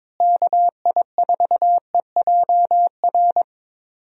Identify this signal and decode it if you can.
Morse code, heard as a keyed tone, about 22 wpm, 700 Hz. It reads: KI4EJR
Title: USA Call Signs - 6 Characters - Repeated 3x - 22wpm